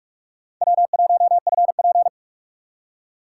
Morse code, heard as a keyed tone, about 41 wpm, 700 Hz. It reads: W1FP
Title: USA Call Signs - 4 Characters - 4x Character Spacing - 45wpm